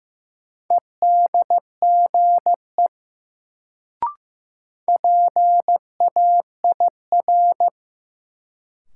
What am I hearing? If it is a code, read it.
Morse code, heard as a keyed tone, about 15 wpm, 700 Hz. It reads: EDGE PAIR
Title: Sets of 3 Words - New Word Drill - 15wpm